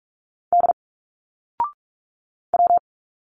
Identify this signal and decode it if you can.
Morse code, heard as a keyed tone, about 46 wpm, 700 Hz. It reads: D W
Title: Single Letters 45wpm